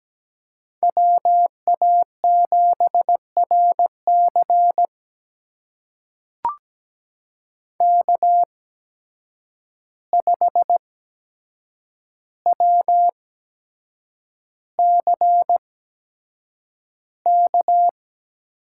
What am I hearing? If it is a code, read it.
Morse code, heard as a keyed tone, about 17 wpm, spaced as for 7 wpm, 700 Hz. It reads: WA7RC K5WCK
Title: USA Call Signs - 5 Characters - 8x Character Spacing - 17wpm